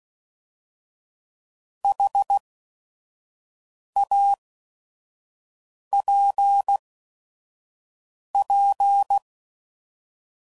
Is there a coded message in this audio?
Yes: Morse code from 1.8 to 9.2 s, heard as a keyed tone, about 16 wpm, spaced as for 5 wpm, 800 Hz. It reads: HAPP